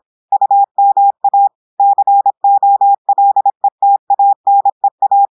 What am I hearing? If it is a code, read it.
Morse code, heard as a keyed tone, about 26 wpm, 800 Hz. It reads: UMA COLETANEA
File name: audios / portugais / portugais texte en clair 1 vitesse 26 mots minutes